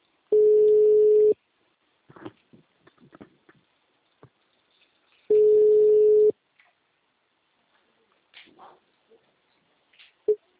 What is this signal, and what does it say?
Ringback tone (Europe, 425 Hz).
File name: out- -2016- - - .60512